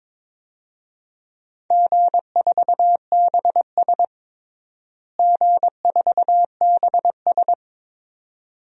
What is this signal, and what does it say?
Morse code, heard as a keyed tone, about 22 wpm, 700 Hz. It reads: G4BS G4BS
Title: DXCC-100 Call Signs - 4 Characters - Repeated 2x - 22wpm